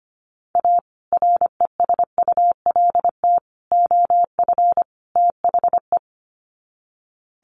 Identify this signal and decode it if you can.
Morse code, heard as a keyed tone, about 25 wpm, 700 Hz. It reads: A RESULT OF THE
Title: Sets of 4 Words 25wpm